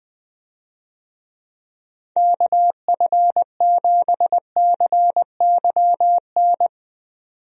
Morse code, heard as a keyed tone, about 20 wpm, 700 Hz. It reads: KF7CYN
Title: USA Call Signs - 6 Characters - 20wpm